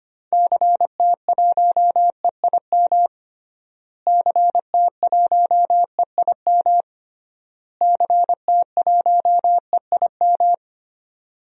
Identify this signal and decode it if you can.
Morse code, heard as a keyed tone, about 25 wpm, 700 Hz. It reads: CT1EIM CT1EIM CT1EIM
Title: DXCC-100 Call Signs - 6 Characters - Repeated 3x - 25wpm